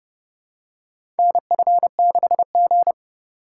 Morse code, heard as a keyed tone, about 31 wpm, 700 Hz. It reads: NF6G
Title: USA Call Signs - 4 Characters - 2x Character Spacing - 30wpm